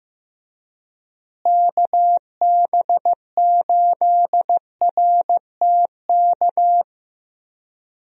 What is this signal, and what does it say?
Morse code, heard as a keyed tone, about 16 wpm, 700 Hz. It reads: KB8RTK
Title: USA Call Signs - 6 Characters - Repeated 2x - 15wpm